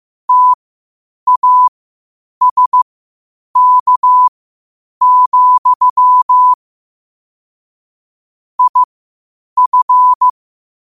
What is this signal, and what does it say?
Morse code, heard as a keyed tone, about 15 wpm, spaced as for 8 wpm, 1000 Hz. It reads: TASK, IF
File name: qotdfiles / 8wpm